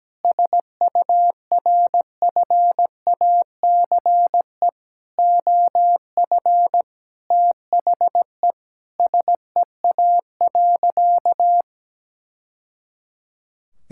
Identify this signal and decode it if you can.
Morse code, heard as a keyed tone, about 17 wpm, 700 Hz. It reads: SURFACE OF THE SEA.
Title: Sentences from Top 1000 Words - Review Part 5 - 17wpm